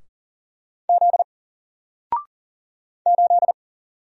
Morse code, heard as a keyed tone, about 40 wpm, 700 Hz. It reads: Z 8
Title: Single Letter-Number 40wpm